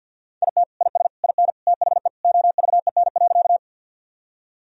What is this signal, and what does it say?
Morse code, heard as a keyed tone, about 50 wpm, 700 Hz. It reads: IT IS IN THE OVEN.
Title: Sentences from Top 1000 Words - Review Part 4 - 50wpm